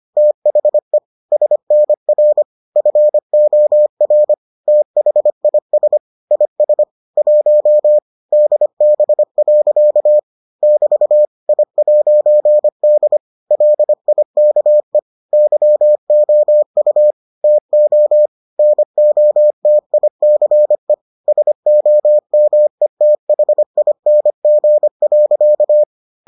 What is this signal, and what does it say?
Morse code, heard as a keyed tone, about 25 wpm, 600 Hz. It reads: THE SNR FOR THIS IS 1 DB. <BT> I'D LIKE YOU TO NOTICE SOMETHING.